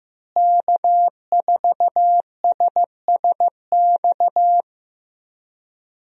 Morse code, heard as a keyed tone, about 15 wpm, 700 Hz. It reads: K4SSX
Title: USA Call Signs - 5 Characters - Repeated 3x - 15wpm